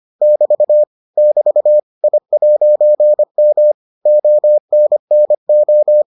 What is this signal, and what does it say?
Morse code, heard as a keyed tone, about 25 wpm, 600 Hz. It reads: <BT> <BT> I'M ONNO